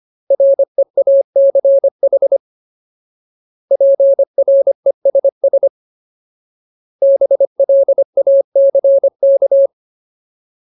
Morse code, heard as a keyed tone, about 25 wpm, 550 Hz. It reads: REACH PRESS BLACK
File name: Ultimate Word List / 25WPM Ultimate